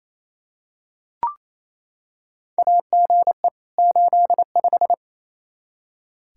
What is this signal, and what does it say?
Morse code, heard as a keyed tone, about 28 wpm, 700 Hz. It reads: AGE 85
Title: Practice QSO - Exchange 3 - More Info - 28wpm